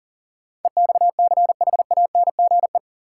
Morse code, heard as a keyed tone, about 40 wpm, 700 Hz. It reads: EXCHANGE
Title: Sets of 5 Words - New Word Drill - 40wpm